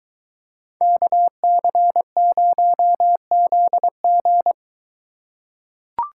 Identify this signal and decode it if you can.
Morse code, heard as a keyed tone, about 22 wpm, 700 Hz. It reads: KC0ZG